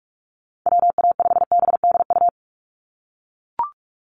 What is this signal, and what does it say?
Morse code, heard as a keyed tone, about 44 wpm, 700 Hz. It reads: WA5BDU
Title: Call Signs - 45wpm